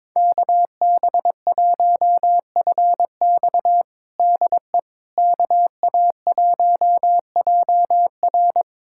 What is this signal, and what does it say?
Morse code, heard as a keyed tone, about 21 wpm, 700 Hz. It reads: KB1FX DE KA1JR
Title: USA Callsign DE Callsign - 5 Characters - 22wpm